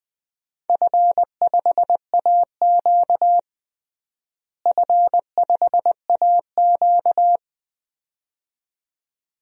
Morse code, heard as a keyed tone, about 20 wpm, 700 Hz. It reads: F5AQ F5AQ
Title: DXCC-100 Call Signs - 4 Characters - Repeated 2x - 20wpm